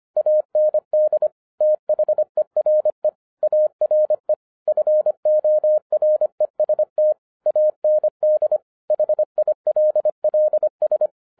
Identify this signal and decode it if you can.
Morse code, heard as a keyed tone, about 25 wpm, 610 Hz. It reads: AND THERE ARE FOREST AND HILLS